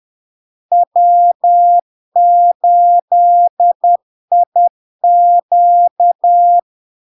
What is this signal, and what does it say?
Morse code, heard as a keyed tone, about 10 wpm, 700 Hz. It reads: W8IQ
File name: LARK-FD-2022-Log-in-CW-10wpm-700-Hz